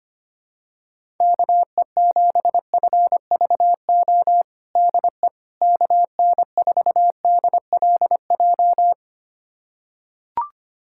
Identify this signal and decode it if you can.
Morse code, heard as a keyed tone, about 25 wpm, 700 Hz. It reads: KE7FVO DE KN4DLJ
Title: USA Callsign DE Callsign - 6 Characters - 25wpm